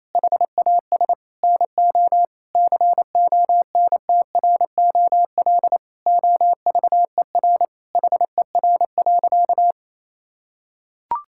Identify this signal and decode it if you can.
Morse code, heard as a keyed tone, about 27 wpm, 700 Hz. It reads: HAS NO CONTROL OVER HER.
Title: Sentences from Top 1000 Words - Review Part 5 - 28wpm